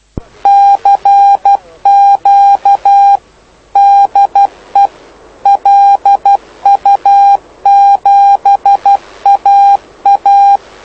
Morse code, heard as a keyed tone, about 12 wpm, 800 Hz. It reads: CQ DE LU7AA
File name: lu4aao / lu7aa 09dic2011 exp planeador vuelo1